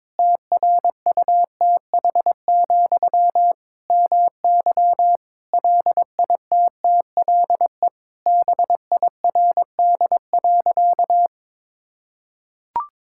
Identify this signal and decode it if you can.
Morse code, heard as a keyed tone, about 22 wpm, 700 Hz. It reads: TRUTH, MY LITTLE BIRD.
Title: Sentences from Top 1000 Words - Review Part 6 - 22wpm